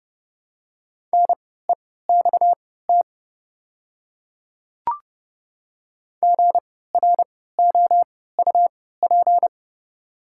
Morse code, heard as a keyed tone, about 29 wpm, spaced as for 17 wpm, 700 Hz. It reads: NEXT GROUP
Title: 3x Character Spacing - Top 300 Words - 30wpm